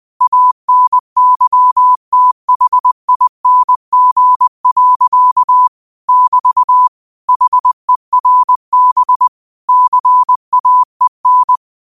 Morse code, heard as a keyed tone, about 20 wpm, 1000 Hz. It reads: ANYTHING. <BT> HERB CAEN
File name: qotdfiles / 20wpm